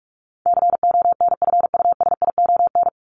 Morse code, heard as a keyed tone, about 44 wpm, 700 Hz. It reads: CONFUSION
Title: Top 1501-1750 Words 45wpm